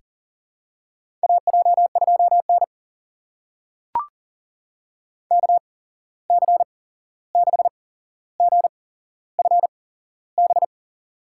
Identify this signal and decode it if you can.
Morse code, heard as a keyed tone, about 40 wpm, spaced as for 15 wpm, 700 Hz. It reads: AJ2N KC6GFB
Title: USA Call Signs - 8x Character Spacing - 40wpm